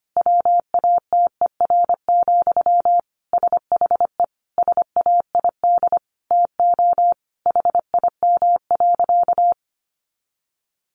Morse code, heard as a keyed tone, about 25 wpm, 700 Hz. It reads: WATER, SHE SAID TO HIM.
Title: Sentences from Top 800 Words 25wpm